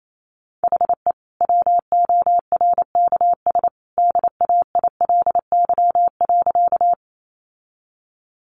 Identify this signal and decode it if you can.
Morse code, heard as a keyed tone, about 28 wpm, 700 Hz. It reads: HE WORKS DAILY.
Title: Sentences from Top 1100 Words 28wpm